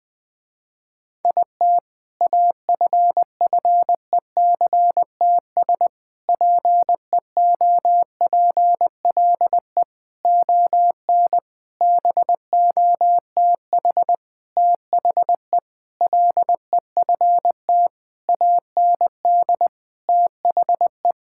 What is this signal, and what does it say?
Morse code, heard as a keyed tone, about 20 wpm, 700 Hz. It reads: IT AFFECTS PEOPLE ON BOTH THE LEFT AND THE
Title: Sentences with 7-Word Phrases - Spoken Beginning - 20wpm